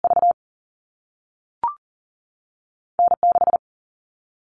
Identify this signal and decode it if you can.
Morse code, heard as a keyed tone, about 40 wpm, spaced as for 3 wpm, 700 Hz. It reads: VN 6
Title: All Cut Numbers - 2-Digit 40wpm